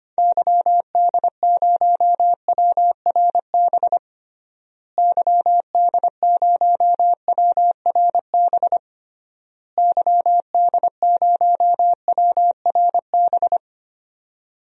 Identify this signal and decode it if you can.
Morse code, heard as a keyed tone, about 25 wpm, 700 Hz. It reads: YD0WRB YD0WRB YD0WRB
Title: DXCC-100 Call Signs - 6 Characters - Repeated 3x - 25wpm